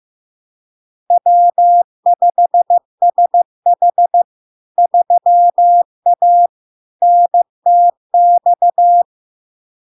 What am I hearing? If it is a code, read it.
Morse code, heard as a keyed tone, about 15 wpm, 700 Hz. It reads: W5SH 3A NTX
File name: LARK-FD-2022-Log-in-CW-15wpm-700-Hz